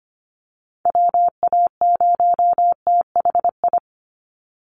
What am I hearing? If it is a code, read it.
Morse code, heard as a keyed tone, about 25 wpm, 700 Hz. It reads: WA0THI
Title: USA Call Signs 25wpm